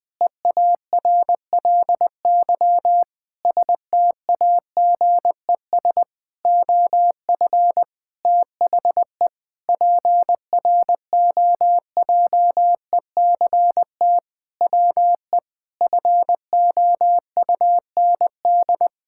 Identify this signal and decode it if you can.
Morse code, heard as a keyed tone, about 20 wpm, 700 Hz. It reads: EARLY STAGES OF THE PROJECT WE FOUND
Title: Sentences with 6-Word Phrases - 20wpm